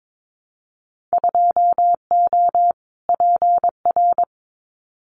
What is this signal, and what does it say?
Morse code, heard as a keyed tone, about 22 wpm, 700 Hz. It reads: 2O PR
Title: Winter Field Day Exchanges Encore 22wpm